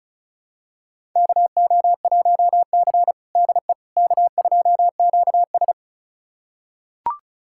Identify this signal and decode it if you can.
Morse code, heard as a keyed tone, about 34 wpm, 700 Hz. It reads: KO1C DE K2QS
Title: USA Callsign DE Callsign - 4 Characters - 35wpm